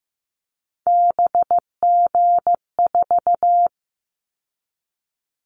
Morse code, heard as a keyed tone, about 15 wpm, 700 Hz. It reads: BG4
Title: Callsign Prefixes for Top 100 DXCC - 15wpm